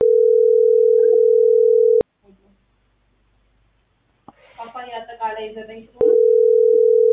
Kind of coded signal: telephone call progress tone